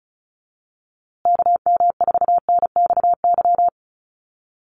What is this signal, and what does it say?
Morse code, heard as a keyed tone, about 34 wpm, 700 Hz. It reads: KM4NXY
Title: USA Call Signs Repeated 2x 35wpm